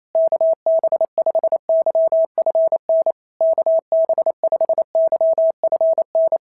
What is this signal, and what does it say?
Morse code, heard as a keyed tone, about 25 wpm, 650 Hz. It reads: KB5YFN KB5YFN
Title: SOTA QSO Practice 28wpm